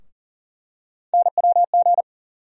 Morse code, heard as a keyed tone, about 40 wpm, 700 Hz. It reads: NWG
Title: USA Call Sign Postfixes - 3 Letters 40wpm